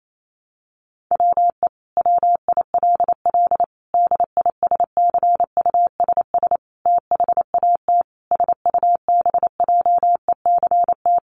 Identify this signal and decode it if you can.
Morse code, heard as a keyed tone, about 28 wpm, 700 Hz. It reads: WE WILL DISCUSS THAT SUBJECT
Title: Sentences from Top 1000 Words 28wpm